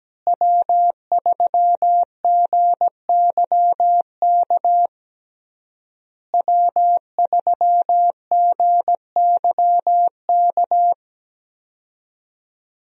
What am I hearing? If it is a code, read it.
Morse code, heard as a keyed tone, about 17 wpm, 700 Hz. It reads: W3GYK W3GYK
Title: USA Call Signs - 5 Characters - Repeated 3x - 17wpm